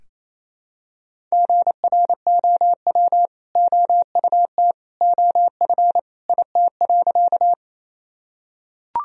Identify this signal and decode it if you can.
Morse code, heard as a keyed tone, about 28 wpm, 700 Hz. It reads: GROW OUT OF IT.
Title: Sentences from Top 1000 Words - Review Part 1 - 28wpm